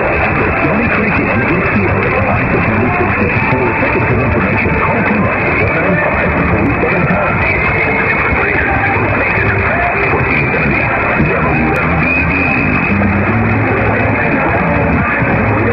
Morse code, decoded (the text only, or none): none